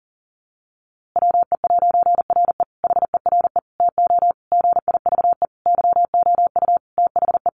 Morse code, heard as a keyed tone, about 40 wpm, 700 Hz. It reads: WE'RE HERE TO GIVE YOU THE
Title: A Sound Of Thunder 40wpm